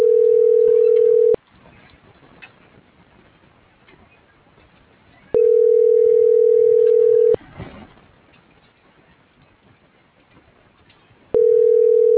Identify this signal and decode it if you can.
Ringback tone (North America), 440 + 480 Hz.